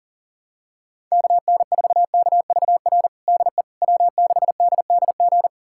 Morse code, heard as a keyed tone, about 40 wpm, 700 Hz. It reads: KN4KVR DE W6DDG
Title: USA Callsign DE Callsign 40wpm